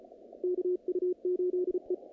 Morse code, heard as a keyed tone, about 34 wpm, 360 Hz, with radio noise. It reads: KU8E
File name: contests / naqp-cw-k5tr-op-k5pi- -2100